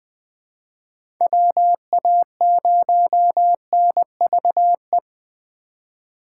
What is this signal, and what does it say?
Morse code, heard as a keyed tone, about 20 wpm, 700 Hz. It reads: WA0NVE
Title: USA Call Signs - 6 Characters - 2x Character Spacing - 20wpm